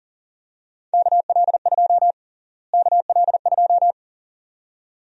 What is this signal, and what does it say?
Morse code, heard as a keyed tone, about 40 wpm, 700 Hz. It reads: KL2 KL2
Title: USA Call Sign Prefixes - Repeated 2x - 40wpm